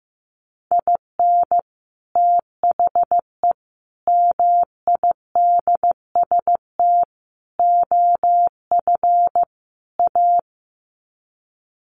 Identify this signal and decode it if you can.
Morse code, heard as a keyed tone, about 15 wpm, 700 Hz. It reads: IN THE MIDST OF A
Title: Sets of 5 Words 15wpm